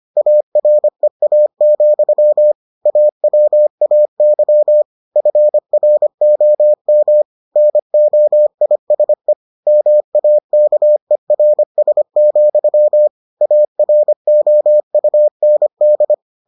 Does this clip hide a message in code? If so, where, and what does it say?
Yes: Morse code from 0.2 to 16.1 s, heard as a keyed tone, about 25 wpm, 600 Hz. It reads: AREA, AWAY FROM NOISE MAKERS, AROUND